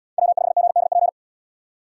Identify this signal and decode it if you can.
Morse code, heard as a keyed tone, about 73 wpm, 700 Hz. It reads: W5GDP